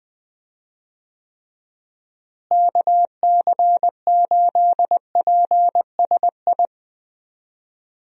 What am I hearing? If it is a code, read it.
Morse code, heard as a keyed tone, about 20 wpm, 700 Hz. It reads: KC8PSI
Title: USA Call Signs - 6 Characters - Repeated 3x - 20wpm